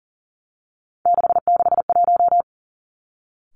Morse code, heard as a keyed tone, about 38 wpm, 700 Hz. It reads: 661
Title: Three Challenging Letter-Number Combinations 40wpm